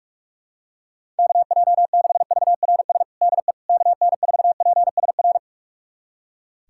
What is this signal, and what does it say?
Morse code, heard as a keyed tone, about 41 wpm, 690 Hz. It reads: KJ6VRS DE KN4PSR
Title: USA Callsign DE Callsign - 6 Characters - 45wpm